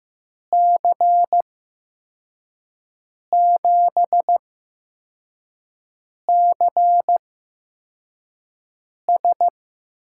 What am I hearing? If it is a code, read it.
Morse code, heard as a keyed tone, about 15 wpm, spaced as for 4 wpm, 700 Hz. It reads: C7CS
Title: USA Call Signs - 8x Character Spacing - 15wpm